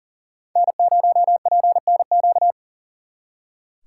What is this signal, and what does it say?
Morse code, heard as a keyed tone, about 52 wpm, 700 Hz. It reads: N0PNQ